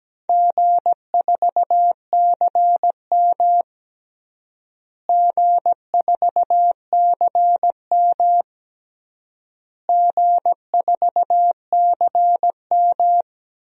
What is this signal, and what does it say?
Morse code, heard as a keyed tone, about 17 wpm, 700 Hz. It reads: G4CM G4CM G4CM
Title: DXCC-100 Call Signs - 4 Characters - Repeated 3x - 17wpm